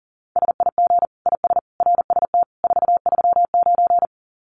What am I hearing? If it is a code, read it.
Morse code, heard as a keyed tone, about 40 wpm, 700 Hz. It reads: SIG IS RST 439
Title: Common QSO Phrases 40wpm